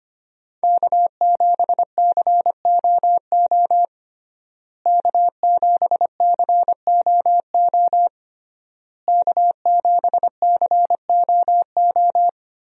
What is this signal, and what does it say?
Morse code, heard as a keyed tone, about 25 wpm, 700 Hz. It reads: K7COO K7COO K7COO
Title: USA Call Signs - 5 Characters - Repeated 3x - 25wpm